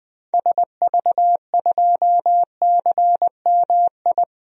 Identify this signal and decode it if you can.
Morse code, heard as a keyed tone, about 21 wpm, 700 Hz. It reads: SV2CMI
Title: DXCC-100 Call Signs - 6 Characters - 20wpm